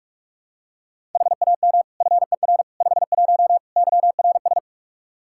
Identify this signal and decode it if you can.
Morse code, heard as a keyed tone, about 47 wpm, 700 Hz. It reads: HAM FER 51 YRS
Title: Practice QSO - Exchange 3 - More Info - 45wpm